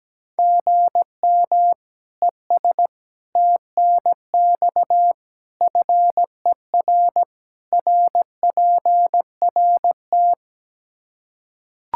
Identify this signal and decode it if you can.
Morse code, heard as a keyed tone, about 17 wpm, 700 Hz. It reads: GM ES TNX FER RPRT
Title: Practice QSO - Exchange 1 - Beginning - 17wpm